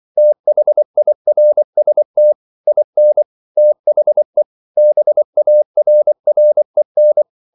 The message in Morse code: THIRST IN THE BARREN

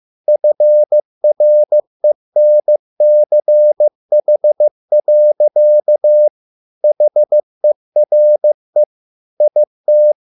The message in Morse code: FRENCH. HERE IT

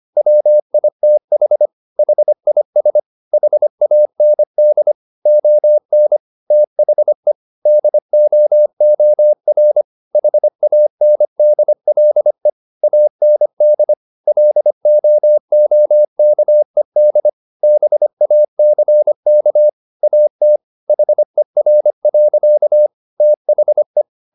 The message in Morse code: WITH HIS HAND ON THE DOOR HANDLE AND LOOKED BACK AT HER. THE

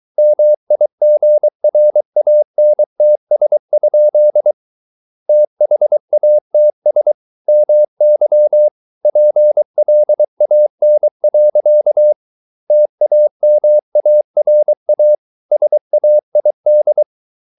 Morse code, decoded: MIGRANTS? THATS MY PLAN. TAMARA SAID

